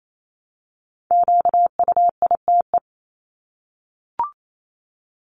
Morse code, heard as a keyed tone, about 28 wpm, 700 Hz. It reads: QUITE
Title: Top 501-750 Words 28wpm